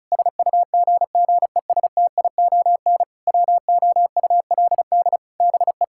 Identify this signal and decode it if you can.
Morse code, heard as a keyed tone, about 36 wpm, 700 Hz. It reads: SUGGESTION WOULD BE
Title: Sentences from Top 1600 Words 35wpm